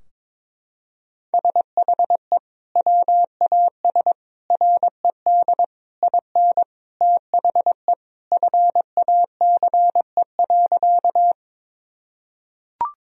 Morse code, heard as a keyed tone, about 22 wpm, 700 Hz. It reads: SHE WAS RED IN THE FACE.
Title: Sentences from Top 1000 Words - Review Part 2 - 22wpm